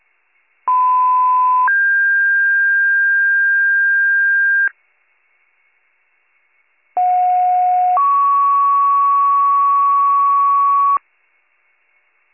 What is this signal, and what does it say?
Two-tone page, heard as a one-second tone, then a three-second tone, about 1010 then 1600 Hz, about 730 then 1120 Hz. It is heard from 0.7 to 11.0 s.